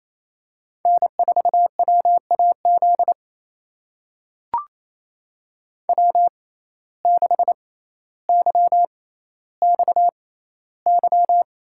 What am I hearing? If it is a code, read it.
Morse code, heard as a keyed tone, about 28 wpm, spaced as for 14 wpm, 700 Hz. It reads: N4WAZ W6YXY
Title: USA Call Signs - 6x Character Spacing - 28wpm